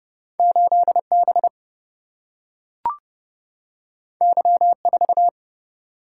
Morse code, heard as a keyed tone, about 29 wpm, 700 Hz. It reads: 8B Y4
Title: Single-Letter Single-Number 30wpm